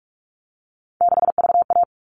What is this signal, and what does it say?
Morse code, heard as a keyed tone, about 44 wpm, 700 Hz. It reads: 6VA
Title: All Cut Numbers - 3-Digit 45wpm